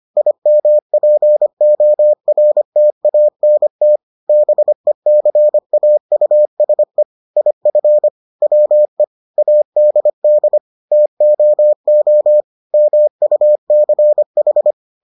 Morse code, heard as a keyed tone, about 25 wpm, 600 Hz. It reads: IMPORTANT BECAUSE IF WE ADD TOO MUCH